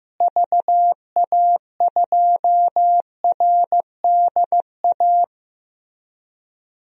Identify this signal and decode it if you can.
Morse code, heard as a keyed tone, about 15 wpm, 700 Hz. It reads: VA2RDA
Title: DXCC-100 Call Signs - 6 Characters - Repeated 3x - 15wpm